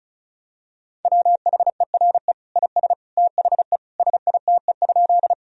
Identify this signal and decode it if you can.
Morse code, heard as a keyed tone, about 36 wpm, 700 Hz. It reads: WHERE IS THE SITE?
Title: Sentences from Top 1000 Words - Review Part 4 - 35wpm